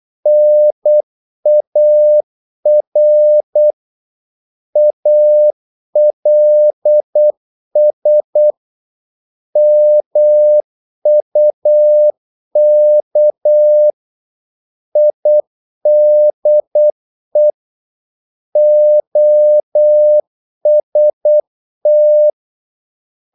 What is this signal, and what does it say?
Morse code, heard as a keyed tone, about 8 wpm, 600 Hz. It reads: NAR ALS MUK IDE OST
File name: Korte ord DK 8wpm